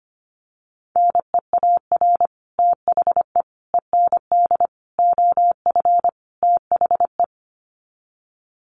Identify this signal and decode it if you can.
Morse code, heard as a keyed tone, about 25 wpm, 700 Hz. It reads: NEAR THE END OF THE
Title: Sets of 5 Words Encore 25wpm